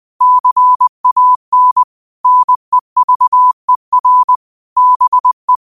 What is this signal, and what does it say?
Morse code, heard as a keyed tone, about 20 wpm, 1000 Hz. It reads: CAN NEVER BE
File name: qotdfiles / 20wpm